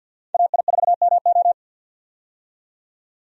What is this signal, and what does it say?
Morse code, heard as a keyed tone, about 51 wpm, 700 Hz. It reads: AI4MO
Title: USA Call Signs - 5 Characters - 50wpm